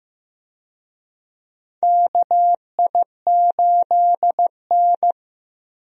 Morse code, heard as a keyed tone, about 15 wpm, 700 Hz. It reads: KI8N
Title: USA Call Signs - 4 Characters - Repeated 2x - 15wpm